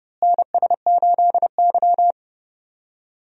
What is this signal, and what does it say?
Morse code, heard as a keyed tone, about 30 wpm, 700 Hz. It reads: NS8Y
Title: USA Call Signs - 4 Characters - Repeated 2x - 30wpm